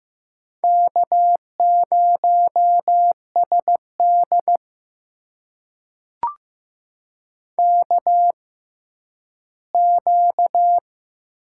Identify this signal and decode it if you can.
Morse code, heard as a keyed tone, about 15 wpm, spaced as for 10 wpm, 700 Hz. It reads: K0SD KQ
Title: USA Call Signs - 4 Characters - 6x Character Spacing - 15wpm